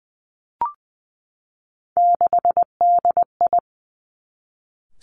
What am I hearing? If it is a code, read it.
Morse code, heard as a keyed tone, about 19 wpm, 700 Hz. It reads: 6DI
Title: Three Challenging Letter-Number Combinations 20wpm